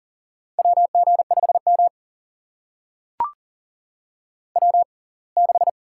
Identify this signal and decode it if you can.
Morse code, heard as a keyed tone, about 40 wpm, spaced as for 27 wpm, 700 Hz. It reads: WG5M W6
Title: USA Call Signs - 4 Characters - 6x Character Spacing - 40wpm